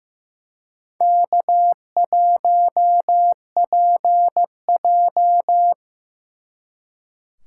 Morse code, heard as a keyed tone, about 18 wpm, 700 Hz. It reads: K1PJ
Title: Speed Racing - USA 1x2 Call Signs 15wpm